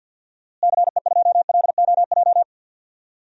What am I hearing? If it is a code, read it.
Morse code, heard as a keyed tone, about 52 wpm, 700 Hz. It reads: CE2LOJ
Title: DXCC-100 Call Signs - 6 Characters - Repeated 3x - 50wpm